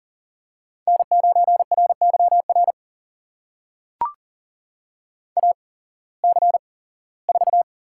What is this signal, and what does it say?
Morse code, heard as a keyed tone, about 40 wpm, spaced as for 21 wpm, 700 Hz. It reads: N9RYR AC4
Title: USA Call Signs - 5 Characters - 8x Character Spacing - 40wpm